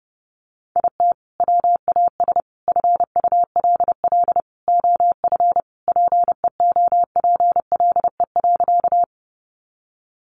Morse code, heard as a keyed tone, about 28 wpm, 700 Hz. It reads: IT WAS FULL OF PEOPLE.
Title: Sentences from Top 500 Words 30wpm